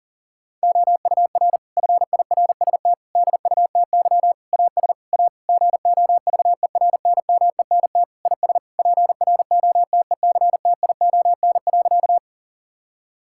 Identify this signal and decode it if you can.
Morse code, heard as a keyed tone, about 40 wpm, 700 Hz. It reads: OUR FIRST DUTY AS A GOVERNMENT IS PROTECTION.
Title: Sentences from Top 1600 Words 40wpm